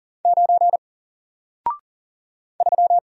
Morse code, heard as a keyed tone, about 40 wpm, 700 Hz. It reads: 9 3
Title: Single Character (Letter:Number:Punctuation:Prosign) 40wpm